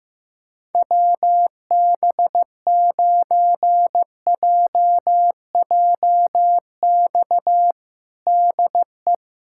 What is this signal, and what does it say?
Morse code, heard as a keyed tone, about 15 wpm, 700 Hz. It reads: WB9JJX DE
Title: USA Callsign DE Callsign - 6 Characters - 15wpm